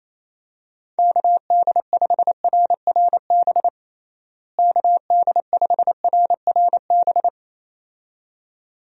Morse code, heard as a keyed tone, about 28 wpm, 700 Hz. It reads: KD5RRB KD5RRB
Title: USA Call Signs - 6 Characters - Repeated 2x - 28wpm